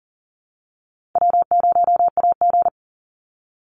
Morse code, heard as a keyed tone, about 40 wpm, 700 Hz. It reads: W0AG
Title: Call Signs Repeated 2x - 40wpm